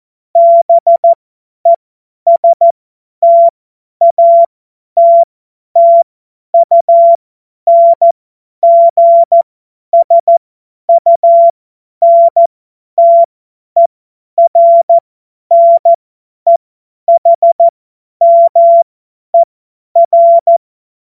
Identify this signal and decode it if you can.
Morse code, heard as a keyed tone, about 14 wpm, spaced as for 10 wpm, 680 Hz. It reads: BESTATTUNGSUNTERNEHMER
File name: Morsekurs / Witze / Witz 27. Januar 2026 50 F70